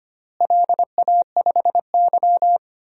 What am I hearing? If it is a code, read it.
Morse code, heard as a keyed tone, about 25 wpm, 700 Hz. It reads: LA5Y
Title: DXCC-100 Call Signs - 4 Characters - Repeated 3x - 25wpm